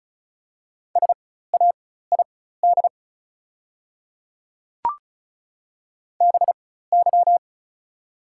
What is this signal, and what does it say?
Morse code, heard as a keyed tone, about 34 wpm, spaced as for 16 wpm, 700 Hz. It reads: SAID BY
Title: 4x Character Spacing - Top 100 Words - 35wpm